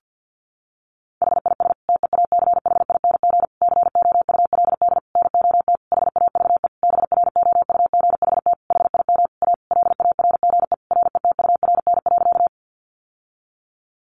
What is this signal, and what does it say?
Morse code, heard as a keyed tone, about 50 wpm, 700 Hz. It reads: HIS TEACHING COULD NOT HAVE BROUGHT HIM A LARGE RETURN.
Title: Sentences from Top 1100 Words 50wpm